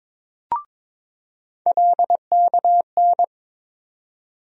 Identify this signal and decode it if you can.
Morse code, heard as a keyed tone, about 22 wpm, 700 Hz. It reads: LKN
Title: USA Call Sign Postfixes - 3 Letters 22wpm